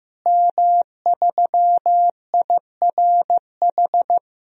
Morse code, heard as a keyed tone, about 15 wpm, 700 Hz. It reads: M3IRH
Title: DXCC-100 Call Signs - 5 Characters - Repeated 2x - 15wpm